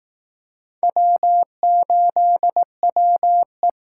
Morse code, heard as a keyed tone, about 19 wpm, 700 Hz. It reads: W8WE